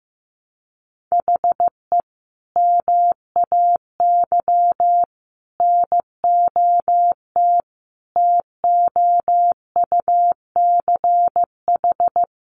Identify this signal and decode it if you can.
Morse code, heard as a keyed tone, about 15 wpm, 700 Hz. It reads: HE MAY NOT TOUCH
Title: Sentences from Top 1500 Words 15wpm